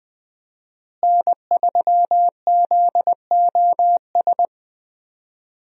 Morse code, heard as a keyed tone, about 21 wpm, 700 Hz. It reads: N3ZOS
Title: USA Call Signs - 5 Characters - Repeated 2x - 20wpm